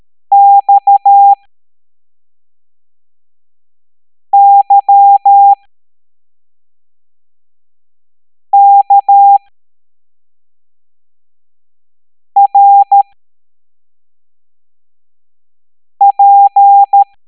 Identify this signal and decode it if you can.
Morse code, heard as a keyed tone, about 13 wpm, spaced as for 3 wpm, 800 Hz. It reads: XYKRP
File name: hamdata / L07-EX02-HQP